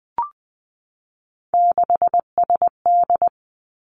Morse code, heard as a keyed tone, about 20 wpm, 700 Hz. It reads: 6SD
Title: Three Challenging Letter-Number Combinations 20wpm